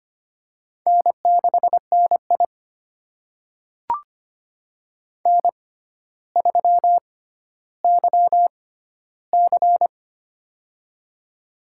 Morse code, heard as a keyed tone, about 25 wpm, spaced as for 13 wpm, 700 Hz. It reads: N6NI N3YC
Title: USA Call Signs - 4 Characters - 6x Character Spacing - 25wpm